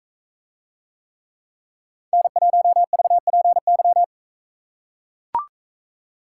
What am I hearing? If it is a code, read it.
Morse code, heard as a keyed tone, about 43 wpm, 700 Hz. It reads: N1VPY